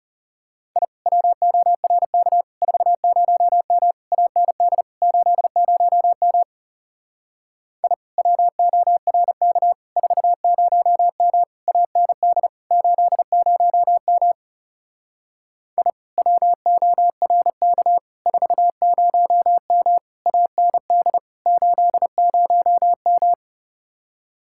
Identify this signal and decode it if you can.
Morse code, heard as a keyed tone, about 34 wpm, 700 Hz. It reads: I WORK 40M AND 80M I WORK 40M AND 80M I WORK 40M AND 80M
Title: Speed-Racing Common QSO Phrases - 30wpm